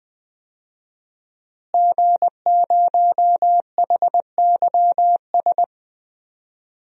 Morse code, heard as a keyed tone, about 20 wpm, 700 Hz. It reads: G0HYS
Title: DXCC-100 Call Signs - 5 Characters - Repeated 3x - 20wpm